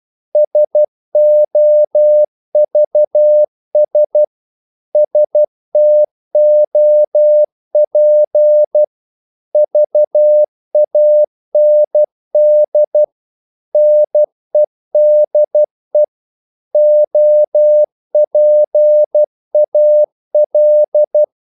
Morse code, heard as a keyed tone, about 12 wpm, 600 Hz. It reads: SOVS STOP VAND NEDE OPAL
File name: Korte ord DK 12wpm